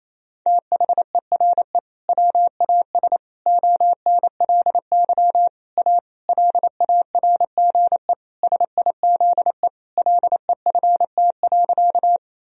Morse code, heard as a keyed tone, about 28 wpm, 690 Hz. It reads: THERE WAS ONLY A LARGE SIZE LEFT.
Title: Sentences from Top 1000 Words - Review Part 2 - 28wpm